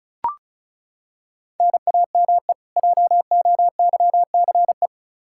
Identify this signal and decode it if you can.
Morse code, heard as a keyed tone, about 36 wpm, 700 Hz. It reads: NAME JOYCE
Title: K1USN SST QSO Practice 35wpm